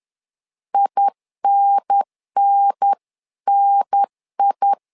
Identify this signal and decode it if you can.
Morse code, heard as a keyed tone, about 10 wpm, 800 Hz. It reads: INNNI
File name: LEZIONI / 3 MWRO / 3 MW 10wpm 1959